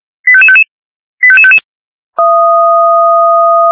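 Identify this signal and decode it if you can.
Five-tone selective call (ZVEI), heard as quick runs of five tones, tones of about 70 ms. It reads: 85555 85555
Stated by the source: SDR-received ZVEI five-tone call 88730.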